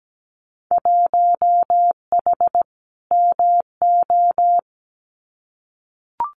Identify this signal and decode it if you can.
Morse code, heard as a keyed tone, about 17 wpm, 700 Hz. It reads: 1H MO
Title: Winter Field Day Exchanges 17wpm